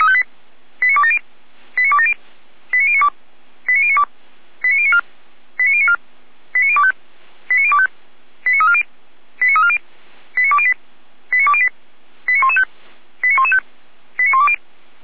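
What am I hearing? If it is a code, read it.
Five-tone selective call (ZVEI), heard as quick runs of five tones, tones of about 70 ms. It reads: 89270 89270 89092 89094 89094 89025 89025 89340 89340 89208 89208 89115 89115 89120
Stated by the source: ZVEI five-tone selective call received on SDR.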